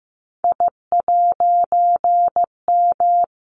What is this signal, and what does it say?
Morse code, heard as a keyed tone, about 15 wpm, 700 Hz. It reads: I'M
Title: A Sound Of Thunder 15wpm